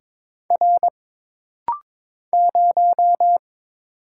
Morse code, heard as a keyed tone, about 22 wpm, 700 Hz. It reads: R 0
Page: This rapid-fire timing is geared to push your ICR (Instant Character Recognition) skills.